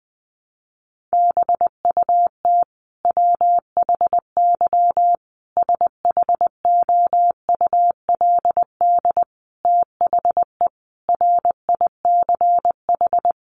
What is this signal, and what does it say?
Morse code, heard as a keyed tone, about 20 wpm, 700 Hz. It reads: BUT WHY SHOULD THE RICH
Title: Sentences from Top 1100 Words 20wpm